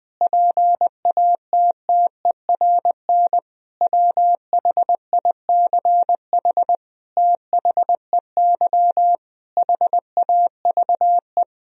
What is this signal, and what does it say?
Morse code, heard as a keyed tone, about 20 wpm, 700 Hz. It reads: PATTERN WHICH THEY HAVE
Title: Sentences from Top 1000 Words - Review Part 2 - 20wpm